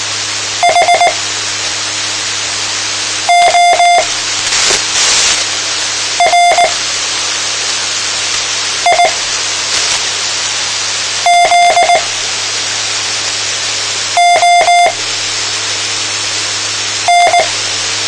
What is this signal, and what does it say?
Morse code, heard as a keyed tone, about 19 wpm, spaced as for 4 wpm, 700 Hz, with noise. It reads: HORIZON
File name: tmranets / Elmer 04 19 2015